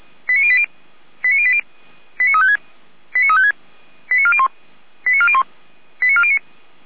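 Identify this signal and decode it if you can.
Five-tone selective call (ZVEI), heard as quick runs of five tones, tones of about 70 ms. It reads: 89080 89080 89356 89356 89401 89401 89409
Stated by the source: ZVEI five-tone call received on SDR.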